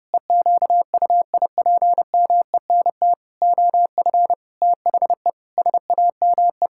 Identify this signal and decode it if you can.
Morse code, heard as a keyed tone, about 30 wpm, 700 Hz. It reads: EQUIPMENT OF THE SAME
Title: Pangram Challenge v2 - 30wpm